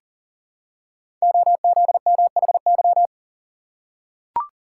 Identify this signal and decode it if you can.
Morse code, heard as a keyed tone, about 43 wpm, 700 Hz. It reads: OZMHY
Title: Code Groups - 5 Letters - 40wpm